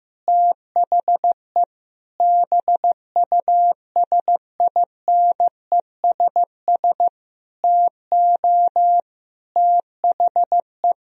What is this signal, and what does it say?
Morse code, heard as a keyed tone, about 15 wpm, 700 Hz. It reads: THE BUSINESS TO THE
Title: Sentences from Top 1000 Words - Review Part 4 - 15wpm